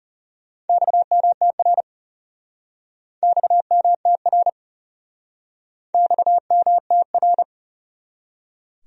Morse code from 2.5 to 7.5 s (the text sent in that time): XMTR XMTR